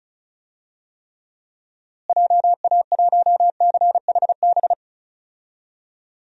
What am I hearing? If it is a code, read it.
Morse code, heard as a keyed tone, about 34 wpm, 700 Hz. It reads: JA1CHB
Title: DXCC-100 Call Signs - 6 Characters - Repeated 3x - 35wpm